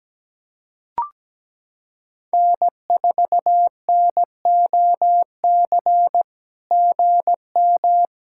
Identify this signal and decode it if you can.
Morse code, heard as a keyed tone, about 17 wpm, 700 Hz. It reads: N4NOC GM
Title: SOTA QSO Practice 17wpm